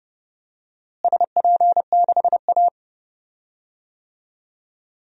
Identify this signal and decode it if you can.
Morse code, heard as a keyed tone, about 30 wpm, 700 Hz. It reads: SP6A
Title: DXCC-100 Call Signs - 4 Characters - Repeated 2x - 30wpm